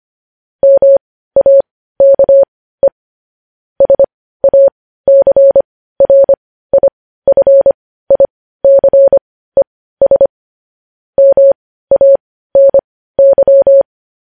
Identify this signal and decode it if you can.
Morse code, heard as a keyed tone, about 25 wpm, spaced as for 15 wpm, 550 Hz. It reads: MAKE SACRIFICES MANY
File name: podcast / News headlines-1x-25x15